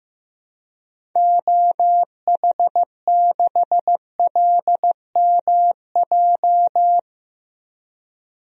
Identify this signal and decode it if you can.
Morse code, heard as a keyed tone, about 15 wpm, 700 Hz. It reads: OH6LMJ
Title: DXCC-100 Call Signs - 6 Characters - Repeated 3x - 15wpm